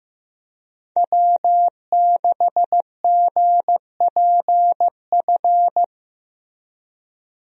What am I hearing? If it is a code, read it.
Morse code, heard as a keyed tone, about 15 wpm, 700 Hz. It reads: W6GPF
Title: USA Call Signs - 5 Characters - Repeated 3x - 15wpm